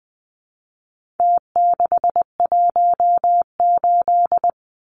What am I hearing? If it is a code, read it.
Morse code, heard as a keyed tone, about 20 wpm, 700 Hz. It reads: T618
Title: Numbers - Contest Style - 4-Digit 20wpm